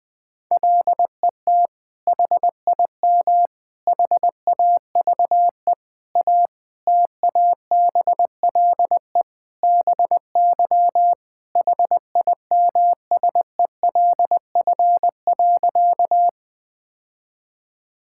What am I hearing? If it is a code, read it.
Morse code, heard as a keyed tone, about 20 wpm, 700 Hz. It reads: LET HIM HAVE A TABLE BY HIMSELF.
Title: Sentences from Top 1600 Words 20wpm